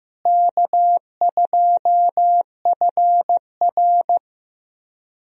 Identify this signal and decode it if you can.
Morse code, heard as a keyed tone, about 15 wpm, 700 Hz. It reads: K2FR
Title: USA Call Signs - 4 Characters - Repeated 2x - 15wpm